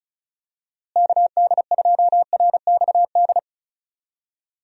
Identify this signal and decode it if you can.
Morse code, heard as a keyed tone, about 36 wpm, 700 Hz. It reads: KD2RXD
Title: USA Call Signs - 6 Characters - Repeated 2x - 35wpm